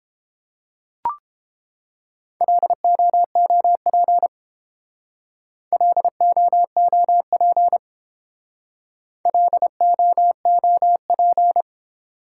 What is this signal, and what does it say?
Morse code, heard as a keyed tone, about 28 wpm, 700 Hz. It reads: LOOP LOOP LOOP